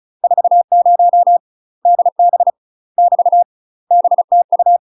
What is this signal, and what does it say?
Morse code, heard as a keyed tone, about 36 wpm, 700 Hz. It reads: 40 DB <BT> BTU
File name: UK MASTER QSOs TRAINING / 35WPM / UK MASTER QSOs 0017